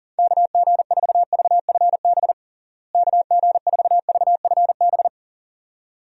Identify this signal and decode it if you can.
Morse code, heard as a keyed tone, about 40 wpm, 700 Hz. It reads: KG4VFB KG4VFB
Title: USA Call Signs - 6 Characters - Repeated 2x - 40wpm